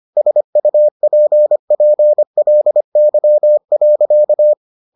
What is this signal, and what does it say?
Morse code, heard as a keyed tone, about 25 wpm, 600 Hz. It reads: SUPPLY.